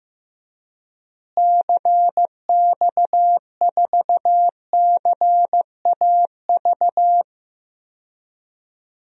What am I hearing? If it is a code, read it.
Morse code, heard as a keyed tone, about 15 wpm, 700 Hz. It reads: CX4CAV
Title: DXCC-100 Call Signs - 6 Characters - Repeated 3x - 15wpm